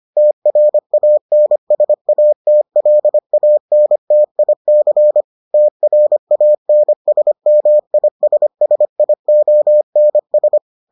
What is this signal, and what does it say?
Morse code, heard as a keyed tone, about 25 wpm, 600 Hz. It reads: TRANSATLANTIC TRANSMISSIONS